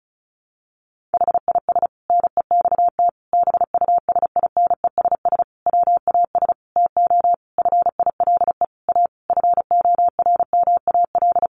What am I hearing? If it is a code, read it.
Morse code, heard as a keyed tone, about 34 wpm, 700 Hz. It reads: HIS NEXT BUSINESS WAS TO FILE A FORMAL
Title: Sentences from Top 1300 Words 35wpm